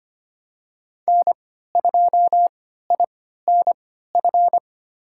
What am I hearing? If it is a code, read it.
Morse code, heard as a keyed tone, about 25 wpm, spaced as for 14 wpm, 700 Hz. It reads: N2INF
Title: USA Call Signs - 5 Characters - 3x Character Spacing - 25wpm